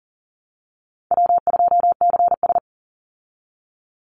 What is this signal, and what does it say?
Morse code, heard as a keyed tone, about 40 wpm, 700 Hz. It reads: W2CS
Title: Easy Call Signs Repeated 3x - 40wpm